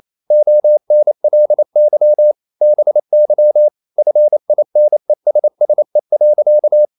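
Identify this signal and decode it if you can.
Morse code, heard as a keyed tone, about 28 wpm, 600 Hz. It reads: ONLY BY FINESSE.
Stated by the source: Morse code practice at 28 wpm, text of Ken Follett's novel Never in English, from CW med Gnister.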